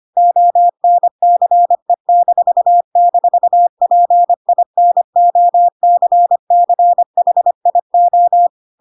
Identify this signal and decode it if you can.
Morse code, heard as a keyed tone, about 25 wpm, 700 Hz. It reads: ONCE--PINOCCHIO
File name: Pinocchio / Pinocchio ch3 25 25 ik1hge